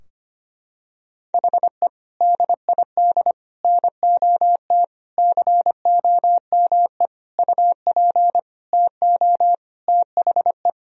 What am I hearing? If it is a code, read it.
Morse code, heard as a keyed tone, about 25 wpm, 700 Hz. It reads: HE DID NOT COME UP TO THE
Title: Sentences from Top 1000 Words - Review Part 3 - 25wpm